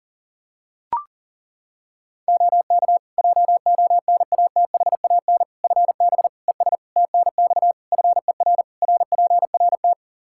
Morse code, heard as a keyed tone, about 40 wpm, 700 Hz. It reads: OK JONATHAN FB ES TNX FER RPRT
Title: Practice QSO - Exchange 2 - Info - 40wpm